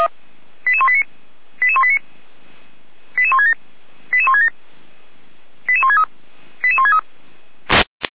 Five-tone selective call (ZVEI), heard as quick runs of five tones, tones of about 70 ms. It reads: 88189 88189 88167 88167 88163 88163